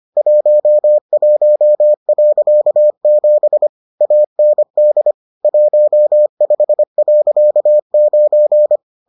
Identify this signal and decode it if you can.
Morse code, heard as a keyed tone, about 25 wpm, 600 Hz. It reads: 11.7 AND 15.9